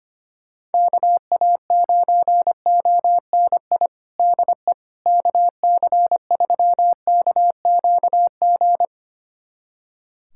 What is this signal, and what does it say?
Morse code, heard as a keyed tone, about 25 wpm, 700 Hz. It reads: KA9ONI DE KC3KQG
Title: USA Callsign DE Callsign - 6 Characters - 25wpm